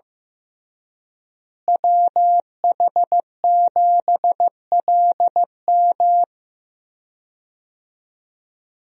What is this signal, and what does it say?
Morse code, heard as a keyed tone, about 15 wpm, 700 Hz. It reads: WH7LM
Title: USA Call Signs - 5 Characters - Repeated 3x - 15wpm